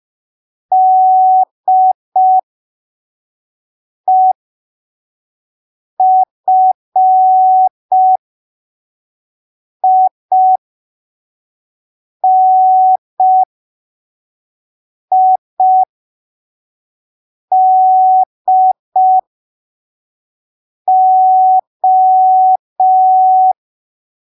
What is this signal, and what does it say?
Morse code, heard as a keyed tone, about 5 wpm, spaced as for 3 wpm, 750 Hz. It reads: DEFINIDO
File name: arquivos / texto / texto 01 5ppm